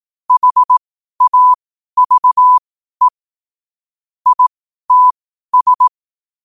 Morse code, heard as a keyed tone, about 18 wpm, spaced as for 12 wpm, 1000 Hz. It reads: HAVE ITS
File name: qotdfiles / 12wpm